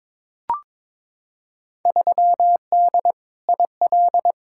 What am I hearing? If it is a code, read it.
Morse code, heard as a keyed tone, about 22 wpm, 700 Hz. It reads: 3D IL
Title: Field Day Exchanges Encore - 22wpm